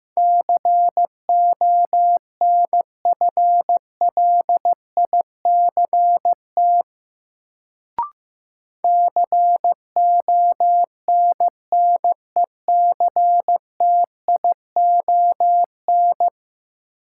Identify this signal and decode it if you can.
Morse code, heard as a keyed tone, about 15 wpm, 700 Hz. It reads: CONFLICT CONNECTION
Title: Sentences from Top 1300 Words - New Word Drill Alt - 15wpm